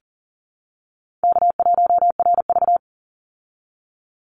Morse code, heard as a keyed tone, about 40 wpm, 700 Hz. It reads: K1RV
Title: Easy Call Signs Repeated 2x - 40wpm